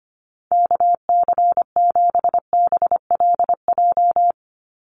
Morse code, heard as a keyed tone, about 25 wpm, 700 Hz. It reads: KC7BLJ
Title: USA Call Signs 25wpm